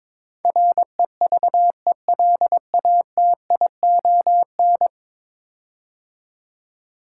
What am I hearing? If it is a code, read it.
Morse code, heard as a keyed tone, about 22 wpm, 700 Hz. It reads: REVELATION
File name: practice / 22wpm / 10 Letter Words - First 3 Letters Given - 22wpm